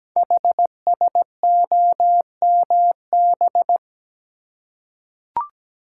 Morse code, heard as a keyed tone, about 17 wpm, 700 Hz. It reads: HSOMB
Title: Code Groups - 5 Letters - 17wpm